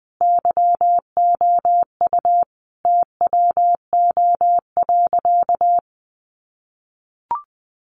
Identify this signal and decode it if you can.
Morse code, heard as a keyed tone, about 20 wpm, 700 Hz. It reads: YOU TWO.
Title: Sentences from Top 700 Words 20wpm